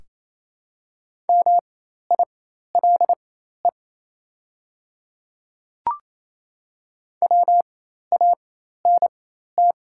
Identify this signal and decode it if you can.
Morse code, heard as a keyed tone, about 28 wpm, spaced as for 13 wpm, 700 Hz. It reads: MILE WANT
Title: 4x Character Spacing - Top 500 Words - 28wpm